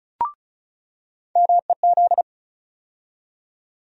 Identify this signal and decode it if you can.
Morse code, heard as a keyed tone, about 34 wpm, 700 Hz. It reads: MEZ
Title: USA Call Sign Postfixes - 3 Letters 35wpm